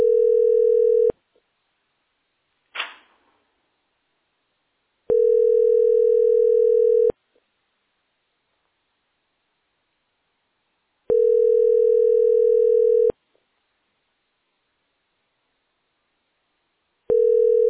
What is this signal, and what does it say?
Ringback tone (North America), 440 + 480 Hz.